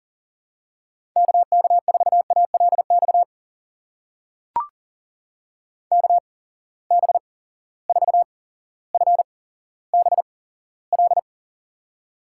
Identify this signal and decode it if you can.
Morse code, heard as a keyed tone, about 41 wpm, spaced as for 17 wpm, 700 Hz. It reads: KK4ALX KB4FBL
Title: USA Call Signs - 6 Characters - 8x Character Spacing - 40wpm